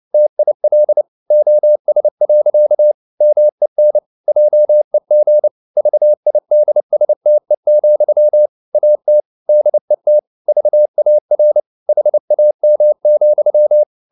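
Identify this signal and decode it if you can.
Morse code, heard as a keyed tone, about 29 wpm, 600 Hz. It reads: TIL OS. MEN JEG VIDSTE, AT DET VAR HAM,
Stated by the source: Fast (QRQ) Morse at 29 wpm, from CW med Gnister.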